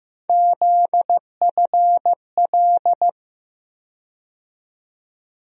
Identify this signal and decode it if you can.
Morse code, heard as a keyed tone, about 15 wpm, 700 Hz. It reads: ZFL
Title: USA Call Sign Postfixes - 3 Letters 15wpm